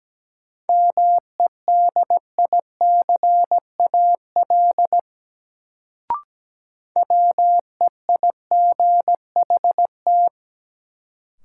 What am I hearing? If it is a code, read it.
Morse code, heard as a keyed tone, about 17 wpm, 700 Hz. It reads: MEDICAL WEIGHT
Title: Sentences from Top 800 Words - New Word Drill - 17wpm